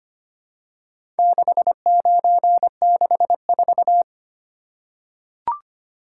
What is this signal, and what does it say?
Morse code, heard as a keyed tone, about 25 wpm, 700 Hz. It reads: 6964
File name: practice / 25wpm / Four Letter-Number Combinations - 25wpm